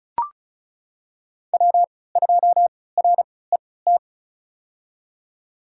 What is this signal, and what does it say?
Morse code, heard as a keyed tone, about 34 wpm, spaced as for 20 wpm, 700 Hz. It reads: W2RET
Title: USA Call Signs - 5 Characters - 3x Character Spacing - 35wpm